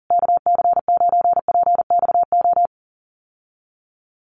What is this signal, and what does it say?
Morse code, heard as a keyed tone, about 40 wpm, 700 Hz. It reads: KC9PXO
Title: USA Call Signs Repeated 3x 40wpm